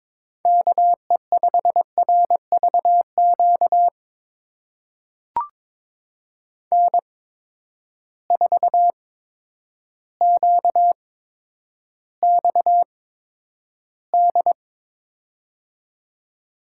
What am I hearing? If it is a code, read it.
Morse code, heard as a keyed tone, about 21 wpm, spaced as for 10 wpm, 700 Hz. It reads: KE5RVQ N4QXD
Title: USA Call Signs - 8x Character Spacing - 22wpm